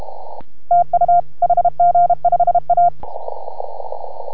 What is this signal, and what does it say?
Morse code, heard as a keyed tone, about 32 wpm, 690 Hz. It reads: TU HG5A